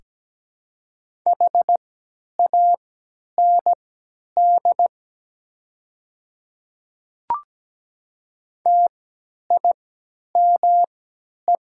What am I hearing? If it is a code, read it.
Morse code, heard as a keyed tone, about 17 wpm, spaced as for 10 wpm, 700 Hz. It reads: HAND TIME
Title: 3x Character Spacing - Top 500 Words - 17wpm